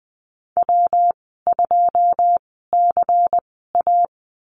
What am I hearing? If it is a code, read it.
Morse code, heard as a keyed tone, about 19 wpm, spaced as for 14 wpm, 700 Hz. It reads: W2CA
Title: 167 Lesson 27g - Number 7 - Call Signs 20wpm